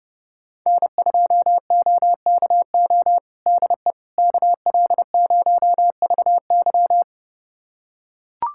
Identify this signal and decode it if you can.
Morse code, heard as a keyed tone, about 30 wpm, 700 Hz. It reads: N2OKO DE KL0VY
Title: USA Callsign DE Callsign 30wpm